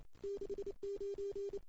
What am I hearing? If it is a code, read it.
Morse code, heard as a keyed tone, about 28 wpm, 410 Hz. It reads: 69
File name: CW / 2008-ss-cw-k5tr-op-n5rz -05:00